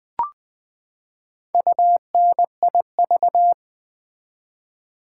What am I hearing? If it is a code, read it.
Morse code, heard as a keyed tone, about 20 wpm, 700 Hz. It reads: UNIV
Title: Build A Word - 4 Letters At A Time - 8-Letter Words - 20wpm